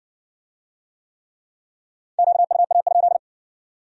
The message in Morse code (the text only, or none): QUA?